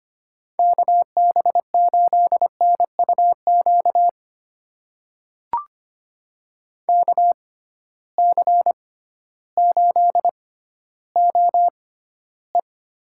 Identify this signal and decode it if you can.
Morse code, heard as a keyed tone, about 25 wpm, spaced as for 13 wpm, 700 Hz. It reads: KB8NUQ KC8OE
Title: USA Call Signs - 6 Characters - 6x Character Spacing - 25wpm